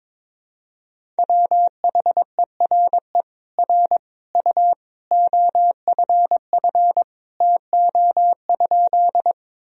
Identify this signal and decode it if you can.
Morse code, heard as a keyed tone, about 22 wpm, 700 Hz. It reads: WHERE R U OFF TO?
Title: Sentences from Top 200 Words and Abbreviations 22wpm